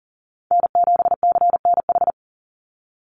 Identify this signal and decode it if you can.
Morse code, heard as a keyed tone, about 40 wpm, 700 Hz. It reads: N7CNH
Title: NA SOTA Chaser Callsigns 40wpm